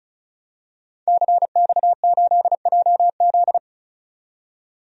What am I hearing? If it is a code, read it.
Morse code, heard as a keyed tone, about 34 wpm, 700 Hz. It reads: CX8JZ
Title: DXCC-100 Call Signs - 5 Characters - 35wpm